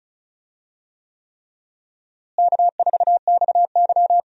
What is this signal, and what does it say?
Morse code, heard as a keyed tone, about 35 wpm, 700 Hz. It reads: K4XY
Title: USA Call Signs - 4 Characters - Repeated 3x - 35wpm